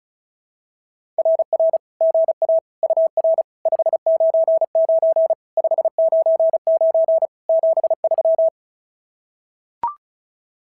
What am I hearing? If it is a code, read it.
Morse code, heard as a keyed tone, about 36 wpm, 660 Hz. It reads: RR GA UR 599 599 73
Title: SOTA QSO Practice 35wpm